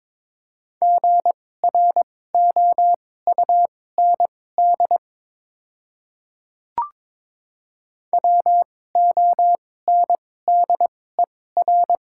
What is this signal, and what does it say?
Morse code, heard as a keyed tone, about 22 wpm, spaced as for 16 wpm, 700 Hz. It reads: GROUND WONDER
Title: 2x Character Spacing - Top 500 Words - 22wpm